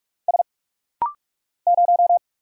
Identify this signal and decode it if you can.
Morse code, heard as a keyed tone, about 46 wpm, 700 Hz. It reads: S 0
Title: Single Letter-Number Alt - Rapid-Fire 45wpm